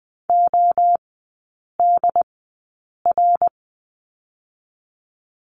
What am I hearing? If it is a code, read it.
Morse code, heard as a keyed tone, about 20 wpm, spaced as for 8 wpm, 700 Hz. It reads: ODR
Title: 121 Lesson 20e - Number 3 - Three Characters 20wpm